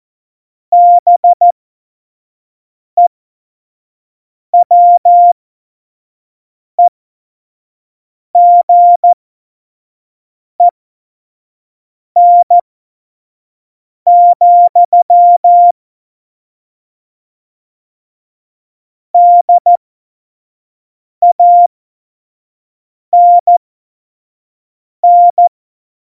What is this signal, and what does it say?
Morse code, heard as a keyed tone, about 14 wpm, spaced as for 5 wpm, 700 Hz. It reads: BEWEGEN, DANN
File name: Morsekurs / Witze / Witz 08. Januar 2019, 25 F70